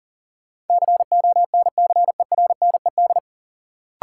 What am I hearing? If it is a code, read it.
Morse code, heard as a keyed tone, about 40 wpm, 700 Hz. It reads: CONCERNED
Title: Sentences from Top 1000 Words - Review Part 6 - New Word Drill - 40wpm